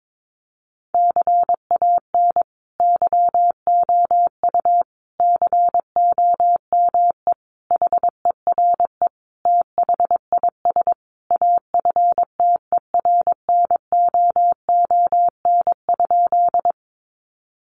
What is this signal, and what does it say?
Morse code, heard as a keyed tone, about 22 wpm, 700 Hz. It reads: CAN YOU COME HERE THIS AFTERNOON?
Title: Sentences from Top 1500 Words 22wpm